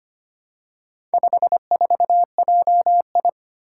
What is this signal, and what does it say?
Morse code, heard as a keyed tone, about 25 wpm, 700 Hz. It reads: 54JI
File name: practice / 25wpm / Four Letter-Number Combinations - 25wpm